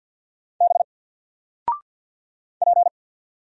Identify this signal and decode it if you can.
Morse code, heard as a keyed tone, about 48 wpm, 700 Hz. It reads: B P
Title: Single Character (Letter:Number:Punctuation:Prosign) Alt 50wpm